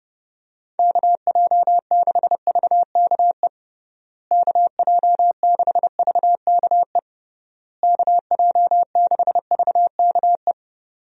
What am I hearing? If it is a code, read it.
Morse code, heard as a keyed tone, about 32 wpm, 700 Hz. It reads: KJ6VKE KJ6VKE KJ6VKE
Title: USA Call Signs - 6 Characters - Repeated 3x - 30wpm